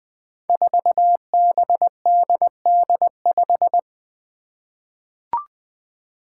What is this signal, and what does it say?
Morse code, heard as a keyed tone, about 20 wpm, 700 Hz. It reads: 4BDD5
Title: Five Challenging Letter-Number Combinations 20wpm